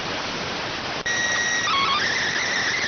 SSTV (slow-scan television), PD 290.